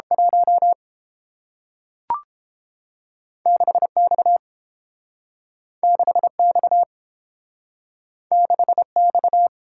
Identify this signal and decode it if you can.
Morse code, heard as a keyed tone, about 30 wpm, 700 Hz. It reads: 1 6X 6X 6X